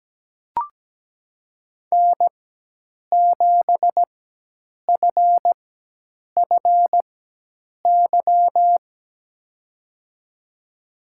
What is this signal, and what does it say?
Morse code, heard as a keyed tone, about 17 wpm, spaced as for 8 wpm, 700 Hz. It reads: N7FFY
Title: USA Call Signs - 5 Characters - 4x Character Spacing - 17wpm